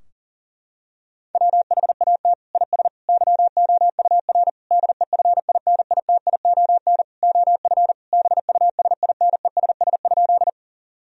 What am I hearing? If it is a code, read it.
Morse code, heard as a keyed tone, about 40 wpm, 700 Hz. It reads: WHAT IS YOUR DEFINITION OF BUSINESS?
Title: Sentences from Top 1000 Words - Review Part 4 - 40wpm